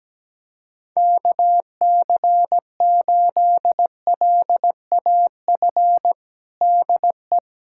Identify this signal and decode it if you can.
Morse code, heard as a keyed tone, about 17 wpm, 700 Hz. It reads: KC8LAF DE
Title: USA Callsign DE Callsign - 6 Characters - 17wpm